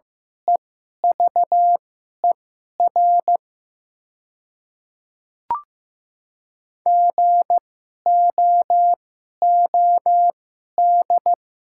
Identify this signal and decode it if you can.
Morse code, heard as a keyed tone, about 15 wpm, spaced as for 11 wpm, 700 Hz. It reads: EVER GOOD
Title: 2x Character Spacing - Top 500 Words - 15wpm